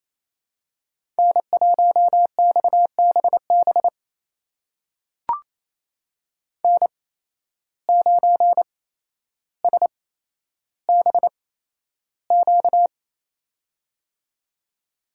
Morse code, heard as a keyed tone, about 28 wpm, spaced as for 12 wpm, 700 Hz. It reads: N1XBB N9SBQ
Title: USA Call Signs - 5 Characters - 8x Character Spacing - 28wpm